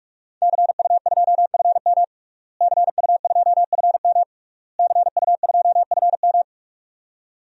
Morse code, heard as a keyed tone, about 46 wpm, 700 Hz. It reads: CU2FM CU2FM CU2FM
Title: DXCC-100 Call Signs - 5 Characters - Repeated 3x - 45wpm